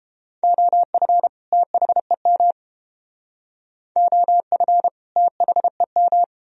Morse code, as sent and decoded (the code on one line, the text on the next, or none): --- ..-. / - .... -- / --- ..-. / - .... --
OF THEM OF THEM